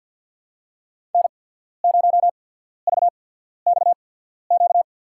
Morse code, heard as a keyed tone, about 48 wpm, spaced as for 14 wpm, 700 Hz. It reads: N0VXQ
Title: USA Call Signs - 5 Characters - 8x Character Spacing - 50wpm